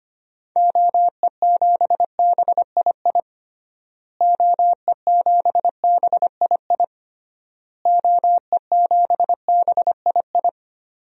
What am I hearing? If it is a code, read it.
Morse code, heard as a keyed tone, about 25 wpm, 700 Hz. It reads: OE7BII OE7BII OE7BII
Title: DXCC-100 Call Signs - 6 Characters - Repeated 3x - 25wpm